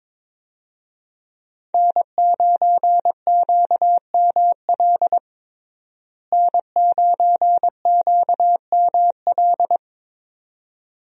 Morse code, heard as a keyed tone, about 22 wpm, 700 Hz. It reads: N9QML N9QML
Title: USA Call Signs - 5 Characters - Repeated 2x - 22wpm